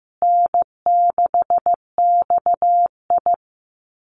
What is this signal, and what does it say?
Morse code, heard as a keyed tone, about 15 wpm, 700 Hz. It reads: N6XI
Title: Easy Call Signs Repeated 3x - 15wpm